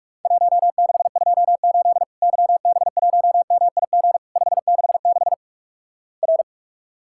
Morse code, heard as a keyed tone, about 47 wpm, 700 Hz. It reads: 1628 YB1MIG 566 R
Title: Worked All Europe DX Contest - QTC Exchange - 45wpm